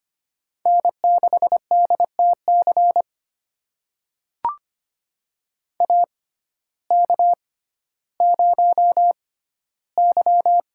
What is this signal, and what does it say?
Morse code, heard as a keyed tone, about 25 wpm, spaced as for 14 wpm, 700 Hz. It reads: N6DTC AK0Y
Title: USA Call Signs - 6x Character Spacing - 25wpm